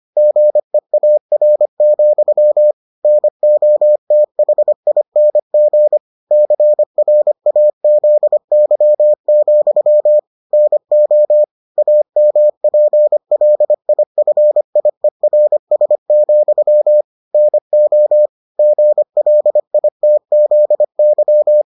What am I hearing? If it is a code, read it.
Morse code, heard as a keyed tone, about 25 wpm, 600 Hz. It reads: GEAR, NOTHING CRAZY, NO AMPLIFIERS, NO GLITZY